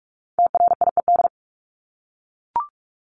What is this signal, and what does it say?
Morse code, heard as a keyed tone, about 44 wpm, 700 Hz. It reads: TRIED
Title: ED Suffix - Top 100 Words with Base Words 45wpm